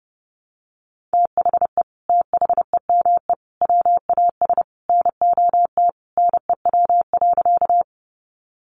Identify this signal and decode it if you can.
Morse code, heard as a keyed tone, about 28 wpm, 700 Hz. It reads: THE THEME WAS NOT NEW.
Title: Sentences from Top 1100 Words 30wpm